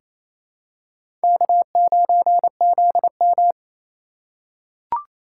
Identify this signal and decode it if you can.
Morse code, heard as a keyed tone, about 28 wpm, 700 Hz. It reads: K9ZM
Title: USA Call Signs - 4 Characters - 2x Character Spacing - 28wpm